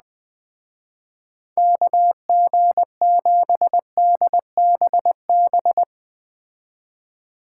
Morse code, heard as a keyed tone, about 20 wpm, 700 Hz. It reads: KG7DBB
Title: Speed Racing - USA 2x3 Call Signs 17wpm